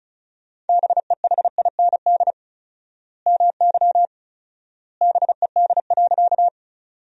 Morse code, heard as a keyed tone, about 36 wpm, 700 Hz. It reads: BEHIND MY BED.
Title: 4x Word-Spacing - Sentences from Top 500 Words - 35wpm